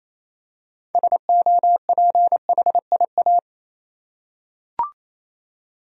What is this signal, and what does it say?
Morse code, heard as a keyed tone, about 28 wpm, 700 Hz. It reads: SOPHIA
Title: USA Common Names - 28wpm